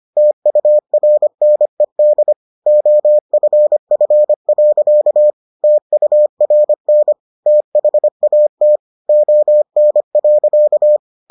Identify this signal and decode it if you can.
Morse code, heard as a keyed tone, about 25 wpm, 600 Hz. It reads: TURNED OFF. TURN THAT ON.